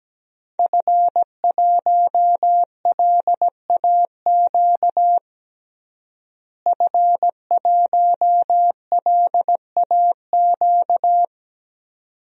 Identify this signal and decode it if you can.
Morse code, heard as a keyed tone, about 17 wpm, 700 Hz. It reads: F1LAQ F1LAQ
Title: DXCC-100 Call Signs - 5 Characters - Repeated 2x - 17wpm